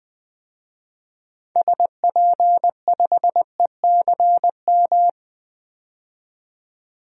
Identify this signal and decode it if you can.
Morse code, heard as a keyed tone, about 21 wpm, 700 Hz. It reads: SP5ECM
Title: DXCC-100 Call Signs - 6 Characters - 20wpm